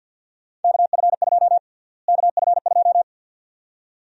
Morse code, heard as a keyed tone, about 52 wpm, 700 Hz. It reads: KF2 KF2
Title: USA Call Sign Prefixes - Repeated 2x - 50wpm